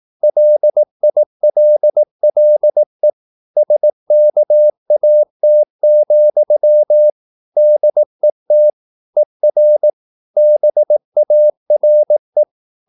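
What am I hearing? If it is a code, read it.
Morse code, heard as a keyed tone, about 18 wpm, 600 Hz. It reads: LILLE SKAT, DET ER BARE
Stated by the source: Slow (QRS) Morse practice at 18 wpm, CW med Gnister.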